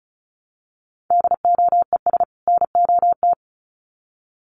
Morse code, heard as a keyed tone, about 34 wpm, 700 Hz. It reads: DOES NOT
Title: Sets of 2 Words 35wpm